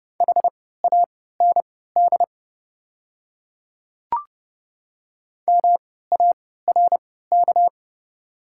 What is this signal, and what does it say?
Morse code, heard as a keyed tone, about 30 wpm, spaced as for 17 wpm, 690 Hz. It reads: HAND MARK
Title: 3x Character Spacing - Top 300 Words - 30wpm